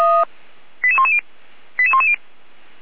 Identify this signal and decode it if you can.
Five-tone selective call (ZVEI), heard as quick runs of five tones, tones of about 70 ms. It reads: 88110 88110
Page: /455/SDR/ZVEI/88102/